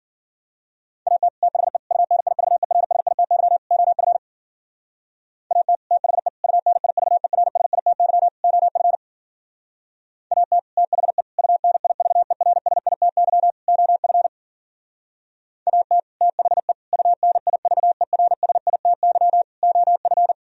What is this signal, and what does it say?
Morse code, heard as a keyed tone, about 51 wpm, 700 Hz. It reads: AT THE UNIVERSITY OF AT THE UNIVERSITY OF AT THE UNIVERSITY OF AT THE UNIVERSITY OF